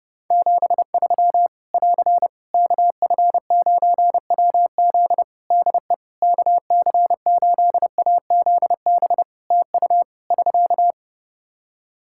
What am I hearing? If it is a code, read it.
Morse code, heard as a keyed tone, about 28 wpm, 700 Hz. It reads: 73 <AR> KF9WZ DE KC8AZB TU <SK>
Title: Practice QSO - Exchange 4 - Ending - 30wpm